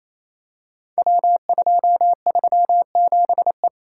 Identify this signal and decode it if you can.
Morse code, heard as a keyed tone, about 27 wpm, 700 Hz. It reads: W237E
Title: Code Groups - 5 Letter:Numbers - 28wpm